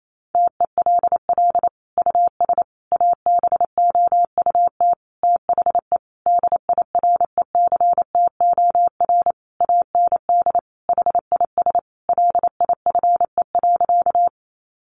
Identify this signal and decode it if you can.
Morse code, heard as a keyed tone, about 28 wpm, 700 Hz. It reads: TELL US ABOUT THE DIRECTOR AND HIS LIFE.
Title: Sentences from Top 1300 Words 28wpm